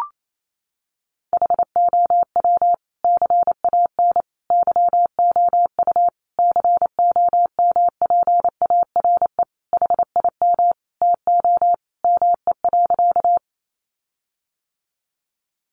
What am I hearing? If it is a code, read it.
Morse code, heard as a keyed tone, about 28 wpm, 700 Hz. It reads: HOW CAN YOU COMPARE HIM TO ME.
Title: Sentences from Top 1300 Words 28wpm